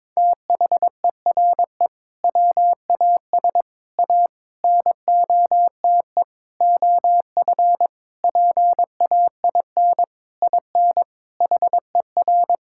Morse code, heard as a keyed tone, about 22 wpm, 700 Hz. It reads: THERE WAS A NOTE OF PAIN IN HER
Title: Sentences from Top 1600 Words 22wpm